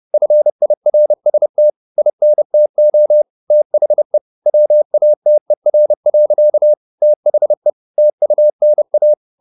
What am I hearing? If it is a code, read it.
Morse code, heard as a keyed tone, about 30 wpm, 600 Hz. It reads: FIRST INTO THE WATER. THE TUNA
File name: the old man and the sea / the old man and the sea 1 30wpm